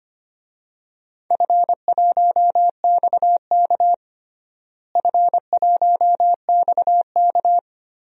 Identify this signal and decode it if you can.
Morse code, heard as a keyed tone, about 25 wpm, 700 Hz. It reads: F1XK F1XK
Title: DXCC-100 Call Signs - 4 Characters - Repeated 2x - 25wpm